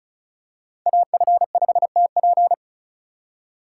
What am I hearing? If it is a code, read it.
Morse code, heard as a keyed tone, about 36 wpm, 700 Hz. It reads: AF5TP
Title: USA Call Signs - 3x Character Spacing - 35wpm